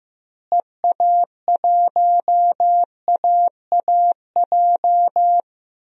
Morse code, heard as a keyed tone, about 15 wpm, 700 Hz. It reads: EA1AAJ
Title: DXCC-100 Call Signs - 6 Characters - Repeated 2x - 15wpm